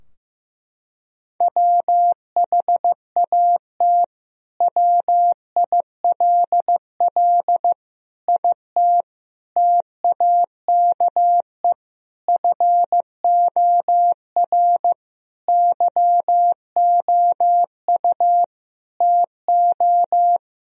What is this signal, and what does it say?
Morse code, heard as a keyed tone, about 16 wpm, 700 Hz. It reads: WHAT WILL IT TAKE FOR YOU TO
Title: Sentences from Top 1000 Words - Review Part 3 - 15wpm